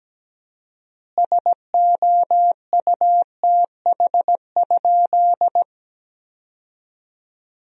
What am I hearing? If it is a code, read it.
Morse code, heard as a keyed tone, about 17 wpm, 700 Hz. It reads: SOUTH?
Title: 4x Word-Spacing - Sentences from Top 400 Words - 17wpm